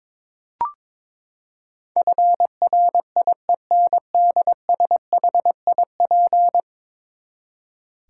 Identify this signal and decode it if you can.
Morse code, heard as a keyed tone, about 22 wpm, 700 Hz. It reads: FRIENDSHIP
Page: morse-code-ninja-practice-22wpm